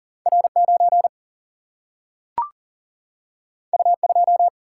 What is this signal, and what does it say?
Morse code, heard as a keyed tone, about 40 wpm, 700 Hz. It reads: R 9 U 2
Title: Single-Letter Single-Number 40wpm